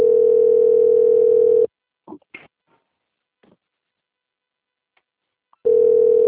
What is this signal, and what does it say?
Ringback tone (North America), 440 + 480 Hz.
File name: rg-1600-unknown- - - .41082